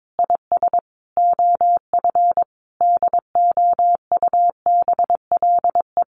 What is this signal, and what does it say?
Morse code, heard as a keyed tone, about 22 wpm, 700 Hz. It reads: IS OF DOUBLE
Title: Sentences from Top 1400 Words 22wpm